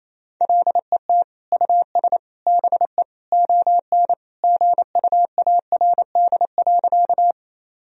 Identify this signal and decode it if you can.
Morse code, heard as a keyed tone, about 28 wpm, 700 Hz. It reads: LET US BE ON GUARD.
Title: Sentences from Top 1000 Words - Review Part 4 - 28wpm